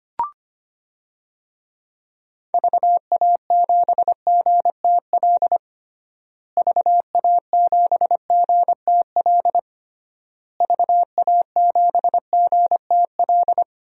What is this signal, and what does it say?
Morse code, heard as a keyed tone, about 25 wpm, 700 Hz. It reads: VA7GTL VA7GTL VA7GTL
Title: DXCC-100 Call Signs - 6 Characters - Repeated 3x - 25wpm